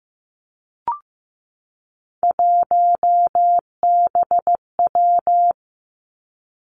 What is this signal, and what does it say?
Morse code, heard as a keyed tone, about 15 wpm, 700 Hz. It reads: 1BW
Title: Three Challenging Letter-Number Combinations 15wpm